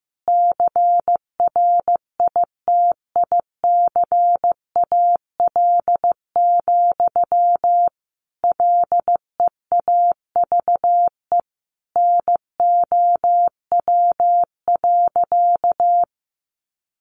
Morse code, heard as a keyed tone, about 15 wpm, 700 Hz. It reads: CRITICAL, LEAVE NOW.
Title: Sentences from Top 1100 Words 15wpm